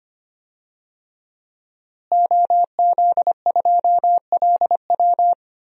Morse code, heard as a keyed tone, about 25 wpm, 700 Hz. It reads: OZ2LW
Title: DXCC-100 Call Signs - 5 Characters - Repeated 3x - 25wpm